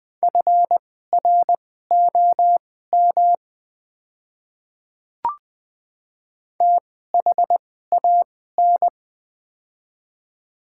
Morse code, heard as a keyed tone, about 20 wpm, spaced as for 14 wpm, 700 Hz. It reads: FROM THAN
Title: 2x Character Spacing - Top 100 Words - 20wpm